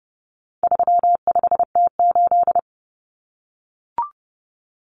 Morse code, heard as a keyed tone, about 30 wpm, 700 Hz. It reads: 35T8
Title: Common Cut Numbers - 4-Digit 30wpm